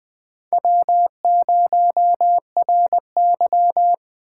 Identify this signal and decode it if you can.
Morse code, heard as a keyed tone, about 20 wpm, 700 Hz. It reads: W0RY
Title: USA Call Signs - 4 Characters - 2x Character Spacing - 20wpm